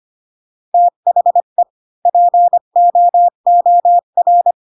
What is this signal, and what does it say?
Morse code, heard as a keyed tone, about 25 wpm, 700 Hz. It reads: THE POOR
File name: Pinocchio / Pinocchio ch1 20 25 ik1hge